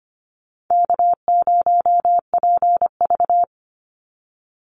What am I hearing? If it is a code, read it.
Morse code, heard as a keyed tone, about 25 wpm, 700 Hz. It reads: K0PV
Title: Easy Call Signs - 25wpm